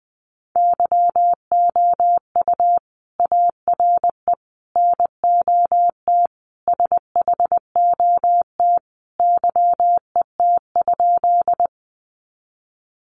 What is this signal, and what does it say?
Morse code, heard as a keyed tone, about 20 wpm, 700 Hz. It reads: YOU ARE NOT SHOT YET?
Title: Sentences from Top 1100 Words 20wpm